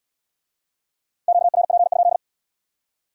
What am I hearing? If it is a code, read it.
Morse code, heard as a keyed tone, about 75 wpm, 700 Hz. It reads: QRP?